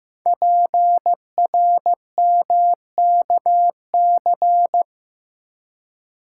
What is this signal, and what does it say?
Morse code, heard as a keyed tone, about 15 wpm, 700 Hz. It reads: PRMKC
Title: Code Groups - 5 Letters - 15wpm